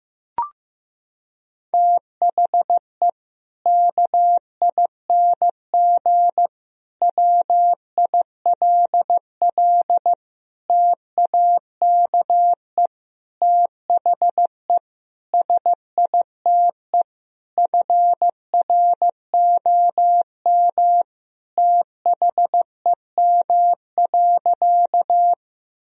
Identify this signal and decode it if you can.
Morse code, heard as a keyed tone, about 15 wpm, 700 Hz. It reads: THE KING WILL TAKE THE SITE FROM THEM.
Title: Sentences from Top 1000 Words - Review Part 4 - 15wpm